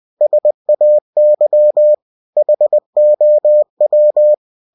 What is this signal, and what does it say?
Morse code, heard as a keyed tone, about 20 wpm, 600 Hz. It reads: SAY HOW
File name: cw-books / winnie-the-pooh / winnie-20wpm-page-0026